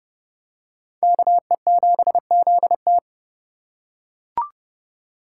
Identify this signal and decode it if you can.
Morse code, heard as a keyed tone, about 29 wpm, 700 Hz. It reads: KE7ZT